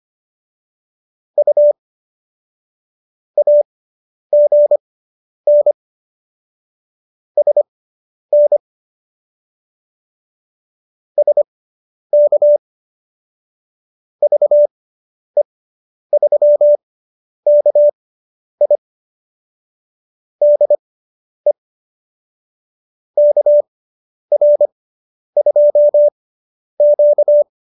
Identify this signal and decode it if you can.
Morse code, heard as a keyed tone, about 25 wpm, spaced as for 10 wpm, 590 Hz. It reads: U AGN SN SK VE3KI DE KR2Q
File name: QSO220 10